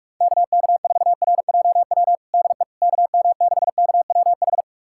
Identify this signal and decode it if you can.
Morse code, heard as a keyed tone, about 45 wpm, 700 Hz. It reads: KK4RJW DE KM6KWH
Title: USA Callsign DE Callsign 45wpm